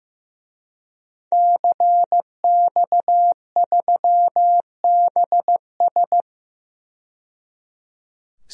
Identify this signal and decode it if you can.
Morse code, heard as a keyed tone, about 15 wpm, 700 Hz. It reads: CX3BS
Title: DXCC-100 Call Signs - 5 Characters - 15wpm